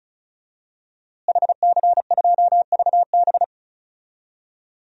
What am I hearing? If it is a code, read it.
Morse code, heard as a keyed tone, about 36 wpm, 700 Hz. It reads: HC2VB
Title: DXCC-100 Call Signs - 5 Characters - Repeated 2x - 35wpm